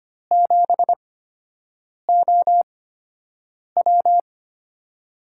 Morse code, heard as a keyed tone, about 25 wpm, spaced as for 7 wpm, 700 Hz. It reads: 7OW
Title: USA Call Signs - 5 Characters - 8x Character Spacing - 25wpm